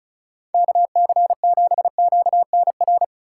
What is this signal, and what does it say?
Morse code, heard as a keyed tone, about 34 wpm, 700 Hz. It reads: KC7QNR
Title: USA Call Signs - 6 Characters - 2x Character Spacing - 35wpm